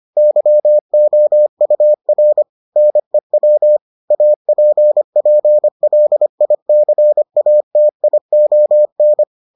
Morse code, heard as a keyed tone, about 25 wpm, 600 Hz. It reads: YOUR NEW APPLICATION